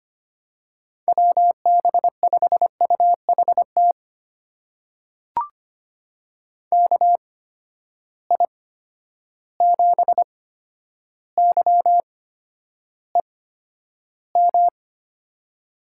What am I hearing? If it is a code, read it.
Morse code, heard as a keyed tone, about 25 wpm, spaced as for 11 wpm, 700 Hz. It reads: WB5UHT KI7YEM
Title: USA Call Signs - 6 Characters - 8x Character Spacing - 25wpm